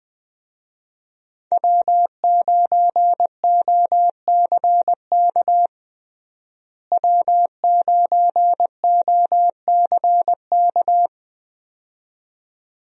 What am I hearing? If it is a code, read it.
Morse code, heard as a keyed tone, about 20 wpm, 700 Hz. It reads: W9OCK W9OCK
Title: USA Call Signs - 5 Characters - Repeated 2x - 20wpm